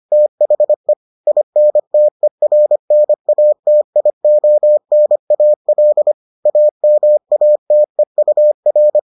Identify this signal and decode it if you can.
Morse code, heard as a keyed tone, about 25 wpm, 600 Hz. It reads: THE INTERNATIONAL AMATEUR